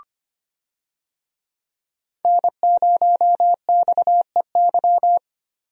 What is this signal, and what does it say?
Morse code, heard as a keyed tone, about 25 wpm, 700 Hz. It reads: N0XEY
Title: USA Call Signs - 5 Characters - 25wpm